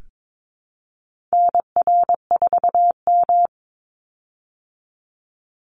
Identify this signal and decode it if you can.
Morse code, heard as a keyed tone, about 22 wpm, 700 Hz. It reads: NR4M
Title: Call Signs Repeated 2x - 22wpm